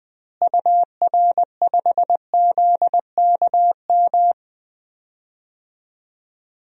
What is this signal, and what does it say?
Morse code, heard as a keyed tone, about 20 wpm, 700 Hz. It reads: UR5ZKM
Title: DXCC-100 Call Signs - 6 Characters - Repeated 3x - 20wpm